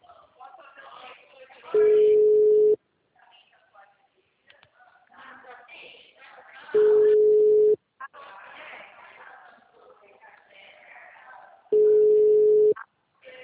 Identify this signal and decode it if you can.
Ringback tone (Europe, 425 Hz), with speech.
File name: out- -2049- - - .81276